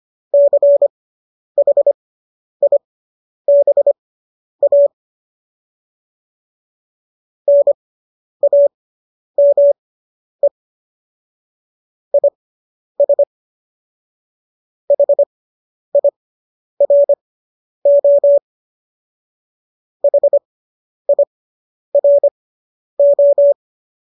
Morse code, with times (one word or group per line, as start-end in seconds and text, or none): 0.3-4.9 CHIBA
7.5-10.5 NAME
12.1-13.2 IS
14.9-18.4 HIRO
20.0-23.5 HIRO